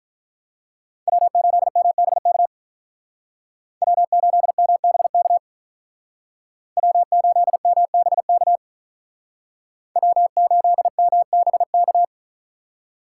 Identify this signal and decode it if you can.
Morse code, heard as a keyed tone, about 44 wpm, 700 Hz. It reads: W8MBK W8MBK W8MBK W8MBK